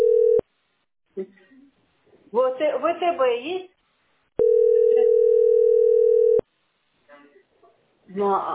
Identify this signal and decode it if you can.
Ringback tone (North America), 440 + 480 Hz, with speech.
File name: fbd33465-f072-44fb-aeff